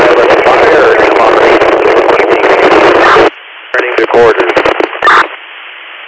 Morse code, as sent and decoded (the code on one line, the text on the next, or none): none